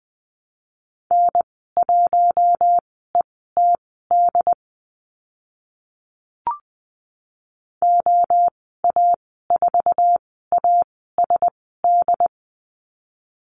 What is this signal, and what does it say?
Morse code, heard as a keyed tone, about 20 wpm, spaced as for 14 wpm, 700 Hz. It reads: N1ETD OA4ASD
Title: 069 Lesson 12g - Letter d - Call Signs 20wpm